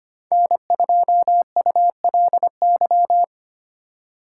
Morse code, heard as a keyed tone, about 25 wpm, 700 Hz. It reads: N2ULY
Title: USA Call Signs - 5 Characters - 2x Character Spacing - 25wpm